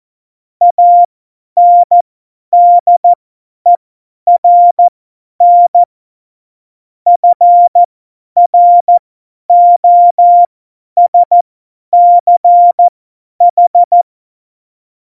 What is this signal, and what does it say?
Morse code, heard as a keyed tone, about 14 wpm, spaced as for 10 wpm, 700 Hz. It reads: ANDERN FROSCH
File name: Morsekurs / Witze / Witz 14. August 2018, 50 F70